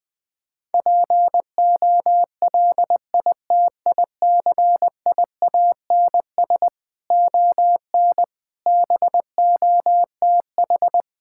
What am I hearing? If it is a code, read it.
Morse code, heard as a keyed tone, about 20 wpm, 700 Hz. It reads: POLITICIANS ON BOTH
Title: Sentences with 6-Word Phrases - 20wpm